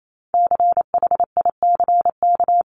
Morse code, heard as a keyed tone, about 28 wpm, 700 Hz. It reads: CHICK
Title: CWT Exchange 28wpm